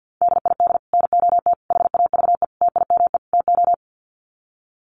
Morse code, heard as a keyed tone, about 50 wpm, 700 Hz. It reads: DID NOT HAVE TIME TO
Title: Sets of 5 Words Encore 50wpm